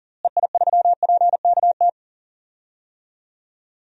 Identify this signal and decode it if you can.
Morse code, heard as a keyed tone, about 40 wpm, 700 Hz. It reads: EI3PKT
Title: DXCC-100 Call Signs - 6 Characters - Repeated 2x - 40wpm